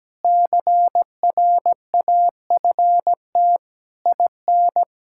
Morse code, heard as a keyed tone, about 17 wpm, 700 Hz. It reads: CRAFT IN
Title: Sentences from Top 1000 Words - Review Part 6 - 17wpm